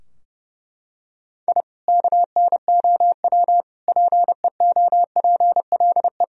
Morse code, heard as a keyed tone, about 32 wpm, 700 Hz. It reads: I KNOW PEOPLE
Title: Sentences from Top 1000 Words - Review Part 4 - 30wpm